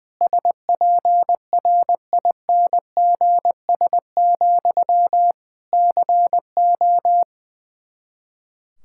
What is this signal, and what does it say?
Morse code, heard as a keyed tone, about 20 wpm, 700 Hz. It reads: SPRINGS, CO
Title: Practice QSO - Exchange 1 - Beginning - 20wpm